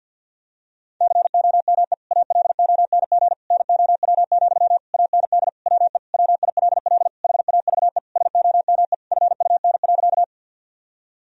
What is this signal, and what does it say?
Morse code, heard as a keyed tone, about 50 wpm, 700 Hz. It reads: COME ALONG NOW, AND WE WILL HAVE SOME FUN.
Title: Sentences from Top 1000 Words - Review Part 4 - 50wpm